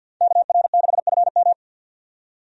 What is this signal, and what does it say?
Morse code, heard as a keyed tone, about 49 wpm, 700 Hz. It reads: KR6FM
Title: USA Call Signs - 5 Characters - 50wpm